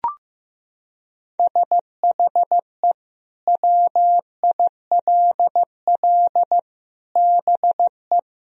Morse code, heard as a keyed tone, about 15 wpm, 700 Hz. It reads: SHE WILL BE
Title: Sentences from Top 1000 Words - Review Part 1 - 15wpm